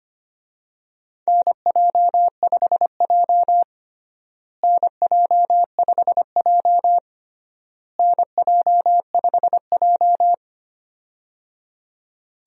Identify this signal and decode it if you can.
Morse code, heard as a keyed tone, about 25 wpm, 700 Hz. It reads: NJ5J NJ5J NJ5J
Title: USA Call Signs - 4 Characters - Repeated 3x - 25wpm